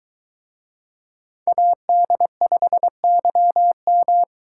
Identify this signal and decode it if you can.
Morse code, heard as a keyed tone, about 22 wpm, 700 Hz. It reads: AD5YM